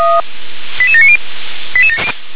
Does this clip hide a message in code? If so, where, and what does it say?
Yes: five-tone selective call (ZVEI) from 0.8 to 1.2 s, heard as quick runs of five tones, tones of about 70 ms. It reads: 88790